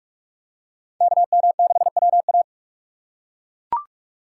Morse code, heard as a keyed tone, about 42 wpm, 700 Hz. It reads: KM6WA